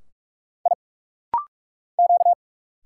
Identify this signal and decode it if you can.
Morse code, heard as a keyed tone, about 44 wpm, 700 Hz. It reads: I Q
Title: Single Letter-Number Alt - Rapid-Fire 45wpm